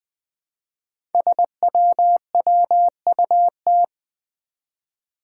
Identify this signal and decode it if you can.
Morse code, heard as a keyed tone, about 20 wpm, 700 Hz. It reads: SWWUT
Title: Five Challenging Letter-Number Combinations 20wpm